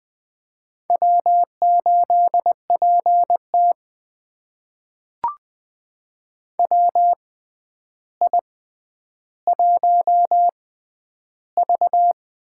Morse code, heard as a keyed tone, about 20 wpm, spaced as for 10 wpm, 700 Hz. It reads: W8PT WI1V
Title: USA Call Signs - 4 Characters - 6x Character Spacing - 20wpm